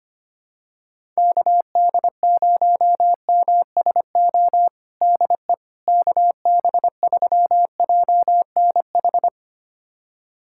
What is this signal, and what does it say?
Morse code, heard as a keyed tone, about 25 wpm, 700 Hz. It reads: KD0MSO DE KB3JNH
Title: USA Callsign DE Callsign - 6 Characters - 25wpm